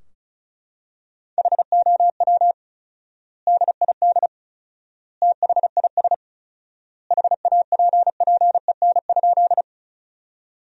Morse code, heard as a keyed tone, about 36 wpm, 700 Hz. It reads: HOW DID THIS HAPPEN?
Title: 4x Word-Spacing - Sentences from Top 400 Words - 35wpm